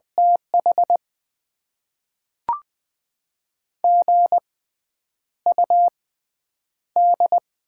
Morse code, heard as a keyed tone, about 20 wpm, spaced as for 9 wpm, 700 Hz. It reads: TH GUD
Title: Top 20 QSO Elements - 6x Character Spacing - 20wpm